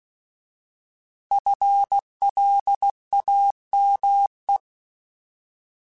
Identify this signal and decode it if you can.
Morse code, heard as a keyed tone, about 16 wpm, 800 Hz. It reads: FLAME
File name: Alphabetical / Alphabetical mp3s / Fs @15